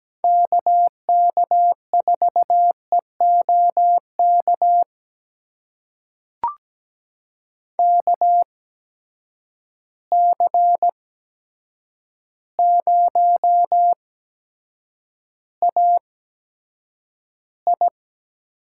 Morse code, heard as a keyed tone, about 17 wpm, spaced as for 8 wpm, 700 Hz. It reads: KK4EOK KC0AI
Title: USA Call Signs - 6 Characters - 8x Character Spacing - 17wpm